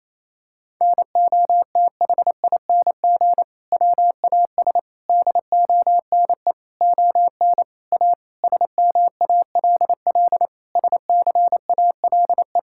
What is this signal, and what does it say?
Morse code, heard as a keyed tone, about 28 wpm, 700 Hz. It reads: NOTHING WAS DONE ON A SMALL SCALE
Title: Sentences from Top 1000 Words - Review Part 5 - 28wpm